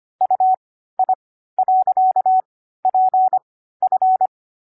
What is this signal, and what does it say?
Morse code, heard as a keyed tone, about 25 wpm, spaced as for 14 wpm, 750 Hz. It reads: UI.PF
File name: CW-Koch / koch.29.3.groups5.15wpm